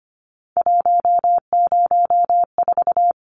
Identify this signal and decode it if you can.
Morse code, heard as a keyed tone, about 25 wpm, 700 Hz. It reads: 104
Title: Numbers - 3-Digit 25wpm